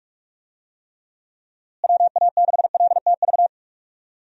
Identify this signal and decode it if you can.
Morse code, heard as a keyed tone, about 44 wpm, 700 Hz. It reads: WA6LTV
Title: USA Call Signs - 6 Characters - Repeated 3x - 45wpm